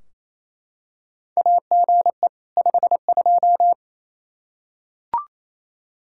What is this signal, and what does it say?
Morse code, heard as a keyed tone, about 28 wpm, 700 Hz. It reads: AGE 52
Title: Practice QSO - Exchange 3 - More Info - 28wpm